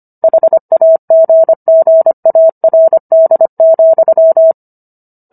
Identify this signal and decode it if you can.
Morse code, heard as a keyed tone, about 25 wpm, 650 Hz. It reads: HAGGARD,